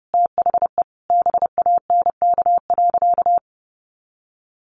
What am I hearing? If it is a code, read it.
Morse code, heard as a keyed tone, about 30 wpm, 700 Hz. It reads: THE BANK.
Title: Sentences from Top 1400 Words 30wpm